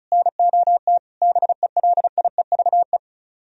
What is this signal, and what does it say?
Morse code, heard as a keyed tone, about 45 wpm, 700 Hz. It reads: NOT BELIEVE